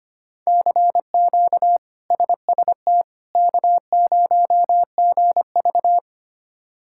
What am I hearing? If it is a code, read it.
Morse code, heard as a keyed tone, about 25 wpm, 700 Hz. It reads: CQ SST K0GV
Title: K1USN SST QSO Practice - 25wpm